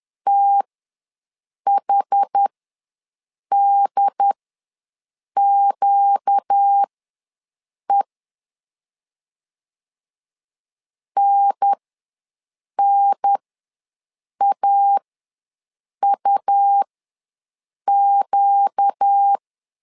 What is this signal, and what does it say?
Morse code, heard as a keyed tone, about 10 wpm, spaced as for 6 wpm, 800 Hz. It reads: THDQE NNAUQ
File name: LEZIONI / 4 Q ws 10wpm 2306